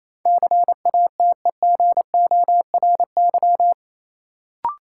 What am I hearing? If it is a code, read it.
Morse code, heard as a keyed tone, about 28 wpm, 700 Hz. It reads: CATEGORY
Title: Sentences from Top 1000 Words - Review Part 7 - New Word Drill - 28wpm